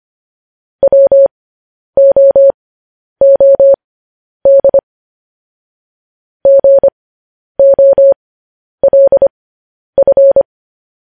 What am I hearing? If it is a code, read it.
Morse code, heard as a keyed tone, about 25 wpm, spaced as for 10 wpm, 550 Hz. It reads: WOOD GOLF
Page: News Headlines in Morse Code at 10 WPM